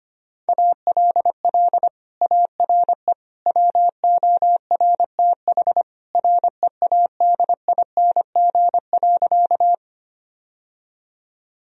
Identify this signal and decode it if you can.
Morse code, heard as a keyed tone, about 25 wpm, 700 Hz. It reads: ALL ARE WORTH READING.
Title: Sentences from Top 1000 Words - Review Part 3 - 25wpm